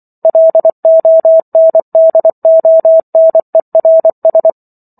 Morse code, heard as a keyed tone, about 24 wpm, 650 Hz. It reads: LONDONERS
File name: Chapter 15 - 24 WPM